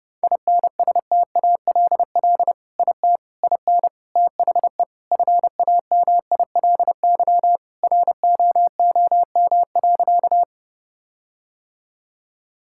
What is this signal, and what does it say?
Morse code, heard as a keyed tone, about 32 wpm, 700 Hz. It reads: INSTALL IT IN THE FAMILY ROOM.
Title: Sentences from Top 1600 Words 30wpm